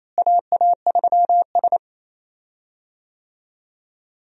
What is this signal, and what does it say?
Morse code, heard as a keyed tone, about 28 wpm, 700 Hz. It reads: AA3S
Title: USA Call Signs - 4 Characters - Repeated 2x - 28wpm